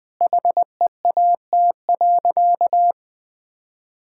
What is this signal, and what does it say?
Morse code, heard as a keyed tone, about 20 wpm, 700 Hz. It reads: HEAT.
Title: Sentences from Top 1000 Words - Review Part 6 - 20wpm